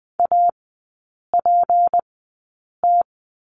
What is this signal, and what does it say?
Morse code, heard as a keyed tone, about 20 wpm, spaced as for 8 wpm, 700 Hz. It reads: APT
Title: 152 Lesson 25e - Letter p - Three Characters 20wpm